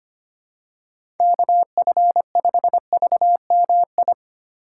Morse code, heard as a keyed tone, about 25 wpm, 700 Hz. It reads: KF5VMI
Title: USA Call Signs - 6 Characters - 3x Character Spacing - 25wpm